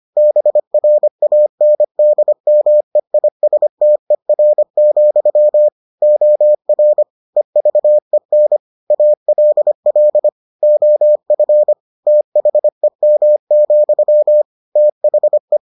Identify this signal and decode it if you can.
Morse code, heard as a keyed tone, about 25 wpm, 600 Hz. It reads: BRANDMEISTER, OR EVEN ALL OF THEM, THE